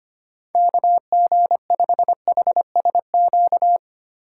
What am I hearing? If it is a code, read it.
Morse code, heard as a keyed tone, about 25 wpm, 700 Hz. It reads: KG5HSQ
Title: USA Call Signs - 6 Characters - 4x Character Spacing - 25wpm